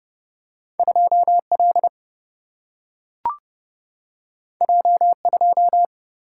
Morse code, heard as a keyed tone, about 30 wpm, 700 Hz. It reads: 2 L J 2
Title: Single-Letter Single-Number 30wpm